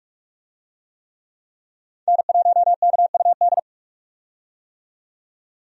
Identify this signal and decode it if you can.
Morse code, heard as a keyed tone, about 46 wpm, 700 Hz. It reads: N1KUD
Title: USA Call Signs - 5 Characters - 45wpm